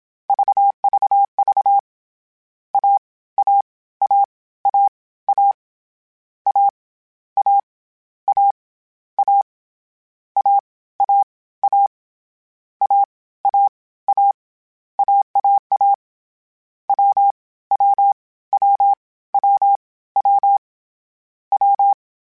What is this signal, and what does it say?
Morse code, heard as a keyed tone, about 27 wpm, 800 Hz. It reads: VVV A A A A A A A A A A A A A A A AAA W W W W W W
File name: CW Code Groups-mp3s / Sets mp3s / AUV sp @25